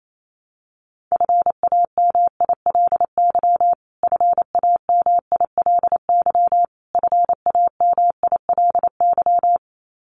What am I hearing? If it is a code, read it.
Morse code, heard as a keyed tone, about 28 wpm, 700 Hz. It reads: FAMILY FAMILY FAMILY
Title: Top 500 Words Repeated 28wpm